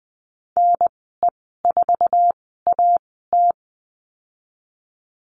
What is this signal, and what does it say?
Morse code, heard as a keyed tone, about 19 wpm, spaced as for 14 wpm, 700 Hz. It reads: NE4AT
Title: 076 Lesson 13g - Letter L - Call Signs 20wpm